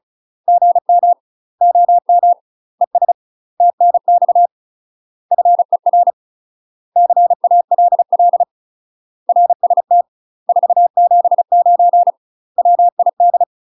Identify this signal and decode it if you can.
Morse code, heard as a keyed tone, about 35 wpm, 700 Hz. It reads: GM OM ES TNX FER CALL RST 479 WID
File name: UK MASTER QSOs TRAINING/35WPM